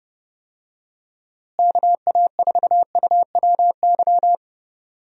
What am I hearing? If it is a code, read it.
Morse code, heard as a keyed tone, about 30 wpm, 700 Hz. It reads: KA4UWY
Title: USA Call Signs - 6 Characters - Repeated 3x - 30wpm